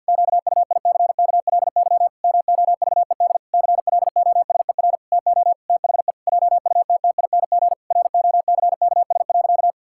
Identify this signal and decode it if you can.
Morse code, heard as a keyed tone, about 50 wpm, 700 Hz. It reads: QUICKLY MOVED CLOSER TO THE JUTTING ROCKS.
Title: Pangram Challenge v2 - 50wpm